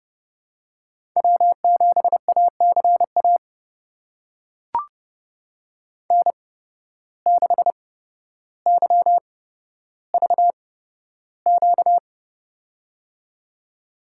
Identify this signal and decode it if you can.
Morse code, heard as a keyed tone, about 30 wpm, spaced as for 13 wpm, 700 Hz. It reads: W7ACA N6YVQ
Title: USA Call Signs - 5 Characters - 8x Character Spacing - 30wpm